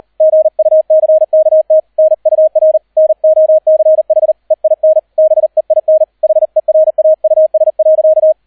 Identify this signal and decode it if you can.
Morse code, heard as a keyed tone, about 38 wpm, 630 Hz, with radio noise. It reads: GUCKT NUR NOCH EIN BEIN HERAUS.